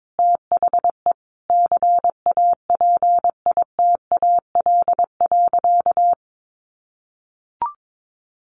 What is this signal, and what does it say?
Morse code, heard as a keyed tone, about 22 wpm, 700 Hz. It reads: THE CAPITAL.
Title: Sentences from Top 1500 Words 22wpm